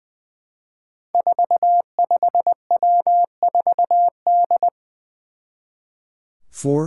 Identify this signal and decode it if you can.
Morse code, heard as a keyed tone, about 20 wpm, 700 Hz. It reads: 45W4D
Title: Five Challenging Letter-Number Combinations 20wpm